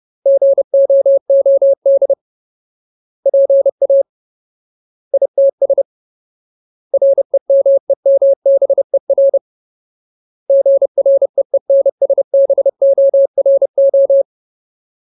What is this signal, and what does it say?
Morse code, heard as a keyed tone, about 30 wpm, 550 Hz. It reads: GOOD PA ITS REMEMBER GREENSBORO
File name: Ultimate Word List / 30WPM Ultimate